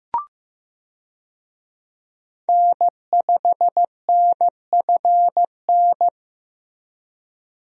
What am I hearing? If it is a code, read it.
Morse code, heard as a keyed tone, about 16 wpm, 700 Hz. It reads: N5NFN
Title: USA Call Signs - 5 Characters - Repeated 3x - 15wpm